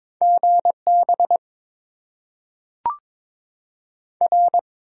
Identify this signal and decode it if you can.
Morse code, heard as a keyed tone, about 22 wpm, 700 Hz. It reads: GB R
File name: Top 90 QSO Elements - 8x Character Spacing - 22wpm